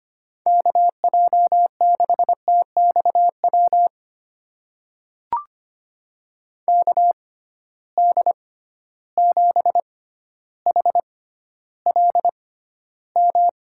Morse code, heard as a keyed tone, about 25 wpm, spaced as for 13 wpm, 700 Hz. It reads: KJ6TXW KD7HLM
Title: USA Call Signs - 6 Characters - 6x Character Spacing - 25wpm